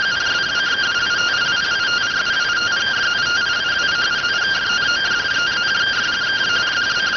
Two-tone teleprinter signal, about 170 Hz shift.